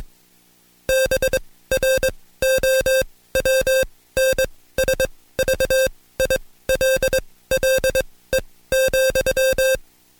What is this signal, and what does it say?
Morse code, heard as a keyed tone, about 21 wpm, spaced as for 16 wpm, 530 Hz. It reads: BROWNSVILLE,
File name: Code Quick / AudioDownld2 / Seg134-Stepper7 15WPM